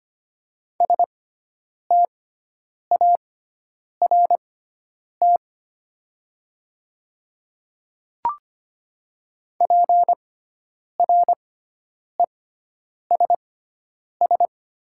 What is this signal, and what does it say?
Morse code, heard as a keyed tone, about 25 wpm, spaced as for 9 wpm, 700 Hz. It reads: START PRESS
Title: 6x Character Spacing - Top 300 Words - 25wpm